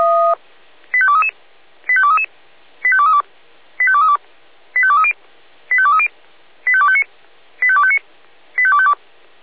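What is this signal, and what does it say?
Five-tone selective call (ZVEI), heard as quick runs of five tones, tones of about 80 ms. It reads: 85233 85233 85232 85232 85230 85230 85259 85259 85252
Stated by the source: ZVEI five-tone call received on SDR.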